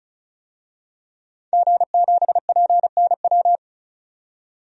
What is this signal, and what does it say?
Morse code, heard as a keyed tone, about 36 wpm, 700 Hz. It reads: G7PNW
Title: DXCC-100 Call Signs - 5 Characters - 35wpm